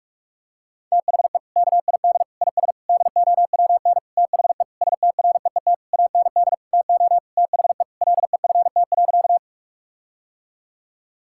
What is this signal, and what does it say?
Morse code, heard as a keyed tone, about 41 wpm, 700 Hz. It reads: THE KID IS DOWN THE STREET AND TO THE LEFT.
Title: Sentences from Top 1600 Words 45wpm